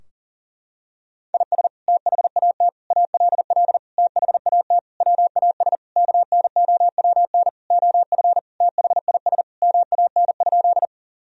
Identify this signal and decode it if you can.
Morse code, heard as a keyed tone, about 37 wpm, 700 Hz. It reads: IS THAT ALL THAT WAS KNOWN OF THIS MAN?
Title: Sentences from Top 1000 Words - Review Part 3 - 40wpm